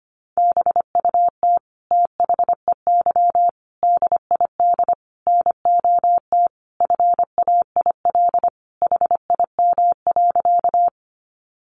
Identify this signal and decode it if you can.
Morse code, heard as a keyed tone, about 25 wpm, 700 Hz. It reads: BUT THEY DID NOT FAIL HIM.
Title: Sentences from Top 1100 Words 25wpm